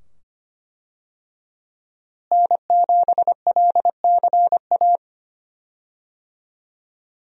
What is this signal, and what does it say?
Morse code, heard as a keyed tone, about 25 wpm, 700 Hz. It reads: N7LCA
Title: USA Call Signs - 5 Characters - Repeated 2x - 25wpm